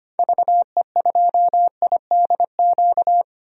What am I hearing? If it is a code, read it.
Morse code, heard as a keyed tone, about 25 wpm, 700 Hz. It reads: VE2IDQ
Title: DXCC-100 Call Signs - 6 Characters - Repeated 2x - 25wpm